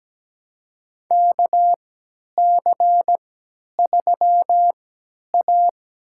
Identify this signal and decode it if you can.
Morse code, heard as a keyed tone, about 17 wpm, spaced as for 10 wpm, 700 Hz. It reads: KC3A
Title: USA Call Signs - 4 Characters - 3x Character Spacing - 17wpm